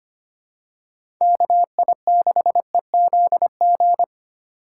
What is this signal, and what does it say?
Morse code, heard as a keyed tone, about 25 wpm, 700 Hz. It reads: KI6EZG
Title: USA Call Signs - 2x Character Spacing - 25wpm